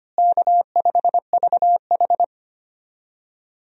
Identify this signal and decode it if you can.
Morse code, heard as a keyed tone, about 25 wpm, 700 Hz. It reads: K5VH
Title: USA Call Signs - 4 Characters - Repeated 3x - 25wpm